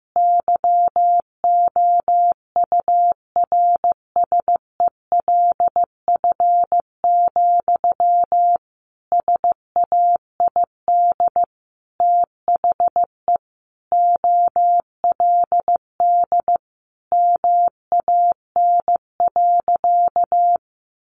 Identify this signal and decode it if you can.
Morse code, heard as a keyed tone, about 15 wpm, 700 Hz. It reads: YOURSELF, SAID THE OLD MAN.
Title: Sentences from Top 1100 Words 15wpm